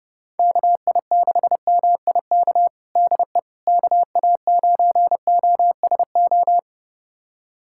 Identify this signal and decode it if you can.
Morse code, heard as a keyed tone, about 32 wpm, 700 Hz. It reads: KI6MIK DE KA9OSO
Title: USA Callsign DE Callsign - 6 Characters - 30wpm